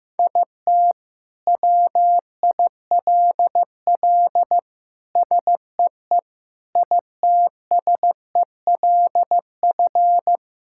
Morse code, heard as a keyed tone, about 15 wpm, 700 Hz. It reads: IT WILL SEE ITSELF
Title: Sentences from Top 1000 Words - Review Part 5 - 15wpm